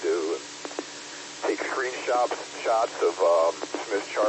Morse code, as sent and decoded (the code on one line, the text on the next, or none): none